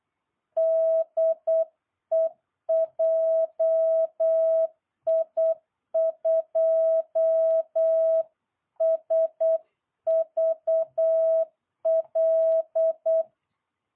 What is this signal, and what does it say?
Morse code, heard as a keyed tone, about 8 wpm, 650 Hz. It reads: DE JI2SVL